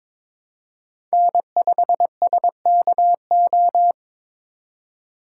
Morse code, heard as a keyed tone, about 26 wpm, 700 Hz. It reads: N5SKO
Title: Speed Racing - USA 1x3 Call Signs 22wpm